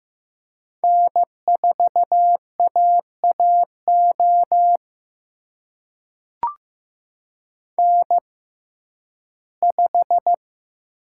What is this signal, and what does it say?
Morse code, heard as a keyed tone, about 15 wpm, spaced as for 11 wpm, 700 Hz. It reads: N4AAO N5
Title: USA Call Signs - 5 Characters - 6x Character Spacing - 15wpm